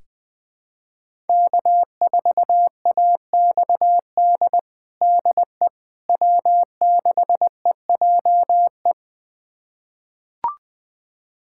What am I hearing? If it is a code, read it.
Morse code, heard as a keyed tone, about 20 wpm, 700 Hz. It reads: K4AXD DE W6EJE
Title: USA Callsign DE Callsign 20wpm